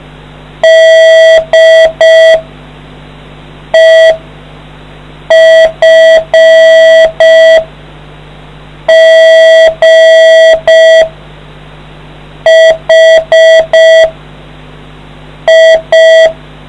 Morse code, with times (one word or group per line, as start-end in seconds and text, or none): 0.6-2.4 D
3.8-4.1 E
5.3-7.6 F
8.9-11.0 G
12.5-14.1 H
15.5-16.3 I